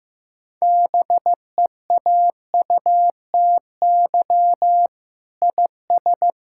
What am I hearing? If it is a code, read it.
Morse code, heard as a keyed tone, about 15 wpm, 700 Hz. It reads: BEAUTY IS
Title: Sentences from Top 1000 Words - Review Part 5 - 15wpm